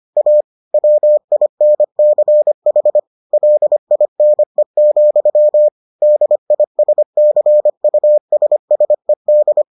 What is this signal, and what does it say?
Morse code, heard as a keyed tone, about 25 wpm, 600 Hz. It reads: A WINCH LINE, DISCUSSED